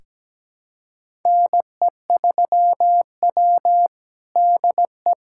Morse code, heard as a keyed tone, about 17 wpm, 700 Hz. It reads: NE3W DE
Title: USA Callsign DE Callsign - 4 Characters - 17wpm